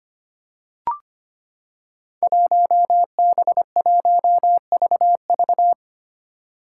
Morse code, heard as a keyed tone, about 25 wpm, 700 Hz. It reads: 1B1VV
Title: Five Challenging Letter-Number Combinations 25wpm